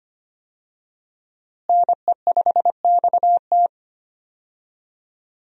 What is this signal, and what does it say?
Morse code, heard as a keyed tone, about 25 wpm, 700 Hz. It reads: NE5XT
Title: USA Call Signs - 5 Characters - Repeated 3x - 25wpm